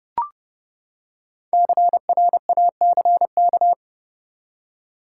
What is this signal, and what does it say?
Morse code, heard as a keyed tone, about 30 wpm, 700 Hz. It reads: CRACK
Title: Five Letter Words - 30wpm